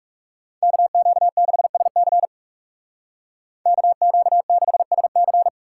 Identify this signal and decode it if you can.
Morse code, heard as a keyed tone, about 36 wpm, 700 Hz. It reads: KQ6SC KQ6SC
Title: Speed Racing - USA 2x2 Call Signs 30wpm